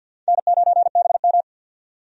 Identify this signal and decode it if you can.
Morse code, heard as a keyed tone, about 50 wpm, 700 Hz. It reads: N9BM
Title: USA Call Signs - 4 Characters - 50wpm